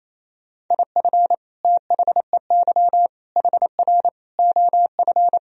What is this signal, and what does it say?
Morse code, heard as a keyed tone, about 27 wpm, 700 Hz. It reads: IF THEY HR OF
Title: Sentences from Top 400 Words and Abbreviations 28wpm